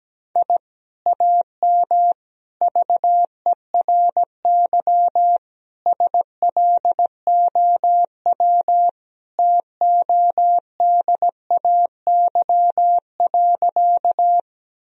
Morse code, heard as a keyed tone, about 17 wpm, 700 Hz. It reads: I AM VERY SLOW TODAY.
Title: Sentences from Top 1000 Words - Review Part 2 - 17wpm